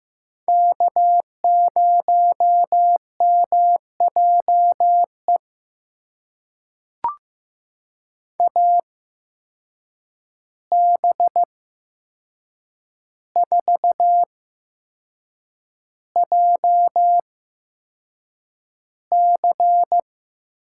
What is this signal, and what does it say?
Morse code, heard as a keyed tone, about 15 wpm, spaced as for 6 wpm, 700 Hz. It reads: K0MJE AB4JC
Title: USA Call Signs - 5 Characters - 8x Character Spacing - 15wpm